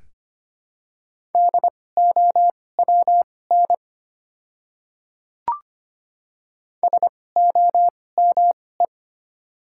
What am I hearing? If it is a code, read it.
Morse code, heard as a keyed tone, about 25 wpm, spaced as for 18 wpm, 700 Hz. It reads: DOWN SOME
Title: 2x Character Spacing - Top 100 Words - 25wpm